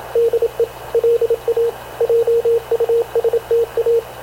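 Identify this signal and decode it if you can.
Morse code, heard as a keyed tone, about 28 wpm, 460 Hz, with radio noise. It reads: DE LA JUSTA
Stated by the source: Havana Radio CLA press (PX) broadcast, recorded by Radio Officer Paul Dumesnil aboard mv CNAV QUEST/CZDO.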